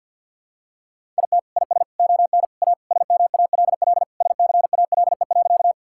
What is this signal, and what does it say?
Morse code, heard as a keyed tone, about 52 wpm, 690 Hz. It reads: IT IS ON A SMALL SCALE.
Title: Sentences from Top 1000 Words - Review Part 5 - 50wpm